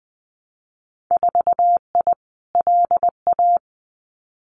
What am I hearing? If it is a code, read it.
Morse code, heard as a keyed tone, about 20 wpm, 700 Hz. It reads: 4I LA
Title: Winter Field Day Exchanges Encore 20wpm